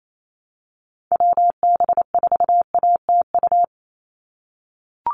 Morse code, heard as a keyed tone, about 28 wpm, 700 Hz. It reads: WB4ATU
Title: USA Call Signs 28wpm